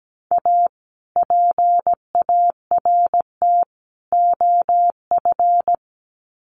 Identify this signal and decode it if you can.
Morse code, heard as a keyed tone, about 17 wpm, 700 Hz. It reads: A PART OF
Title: Sets of 3 Words 17wpm